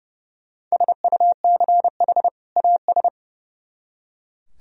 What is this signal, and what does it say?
Morse code, heard as a keyed tone, about 37 wpm, 700 Hz. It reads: SUCH AS
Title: Speed-Racing - Sets of 2 Words 30wpm